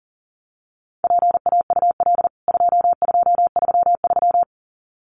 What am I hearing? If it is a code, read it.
Morse code, heard as a keyed tone, about 40 wpm, 700 Hz. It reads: PAUL 2233
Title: CWT Exchange 40wpm